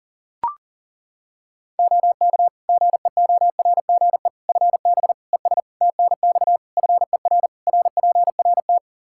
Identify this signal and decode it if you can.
Morse code, heard as a keyed tone, about 37 wpm, 700 Hz. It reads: OK GEORGE FB ES TNX FER RPRT
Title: K1USN SST QSO Practice 40wpm